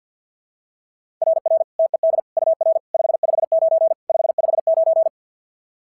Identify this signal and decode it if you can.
Morse code, heard as a keyed tone, about 50 wpm, 690 Hz. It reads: RR TED UR 559 559
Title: SOTA QSO Practice 50wpm